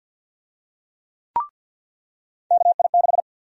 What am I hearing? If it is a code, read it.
Morse code, heard as a keyed tone, about 48 wpm, 700 Hz. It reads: KI6
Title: USA Call Sign Prefixes - 50wpm